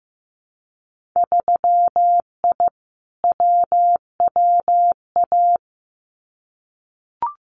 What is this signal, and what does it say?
Morse code, heard as a keyed tone, about 15 wpm, 700 Hz. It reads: 3I WWA
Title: Winter Field Day Exchanges Encore 15wpm